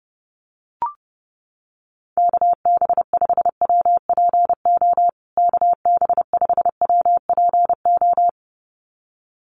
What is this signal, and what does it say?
Morse code, heard as a keyed tone, about 28 wpm, 700 Hz. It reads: KB5WPO KB5WPO
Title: USA Call Signs Repeated 2x 30wpm